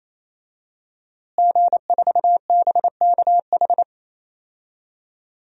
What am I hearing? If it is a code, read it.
Morse code, heard as a keyed tone, about 27 wpm, 700 Hz. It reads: G4BKH
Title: DXCC-100 Call Signs - 5 Characters - Repeated 3x - 28wpm